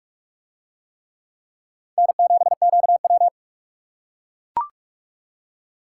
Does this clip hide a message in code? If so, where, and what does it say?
Yes: Morse code from 2.0 to 3.3 s, heard as a keyed tone, about 44 wpm, 700 Hz. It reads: N7QW